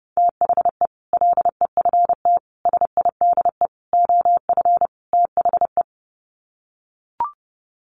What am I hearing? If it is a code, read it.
Morse code, heard as a keyed tone, about 30 wpm, 700 Hz. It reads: THE LEFT SIDE OF THE
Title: Sets of 5 Words Encore 30wpm